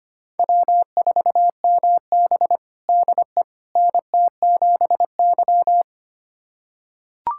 Morse code, heard as a keyed tone, about 25 wpm, 700 Hz. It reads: W4MB DE NT7Y
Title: USA Callsign DE Callsign - 4 Characters - 25wpm